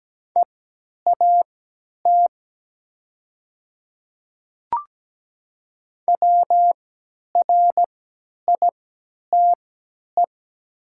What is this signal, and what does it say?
Morse code, heard as a keyed tone, about 17 wpm, spaced as for 10 wpm, 700 Hz. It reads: EAT WRITE
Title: 3x Character Spacing - Top 300 Words - 17wpm